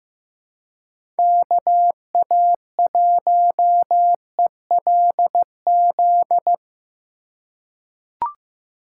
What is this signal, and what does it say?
Morse code, heard as a keyed tone, about 16 wpm, 700 Hz. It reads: KA1ELZ
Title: USA Call Signs - 6 Characters - 4x Character Spacing - 15wpm